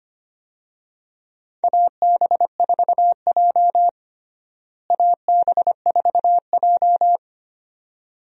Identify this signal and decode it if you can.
Morse code, heard as a keyed tone, about 25 wpm, 700 Hz. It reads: AB4J AB4J
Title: USA Call Signs - 4 Characters - Repeated 2x - 25wpm